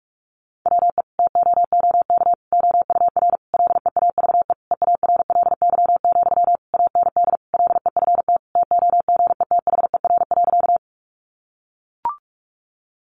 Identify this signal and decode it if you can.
Morse code, heard as a keyed tone, about 44 wpm, 700 Hz. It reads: WE TOOK OUR LEAVE EARLY, AND LEFT TOGETHER.
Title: Sentences from Top 400 Words 45wpm